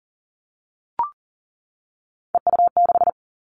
Morse code, heard as a keyed tone, about 40 wpm, 700 Hz. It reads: EU6
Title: Callsign Prefixes for Top 100 DXCC - 40wpm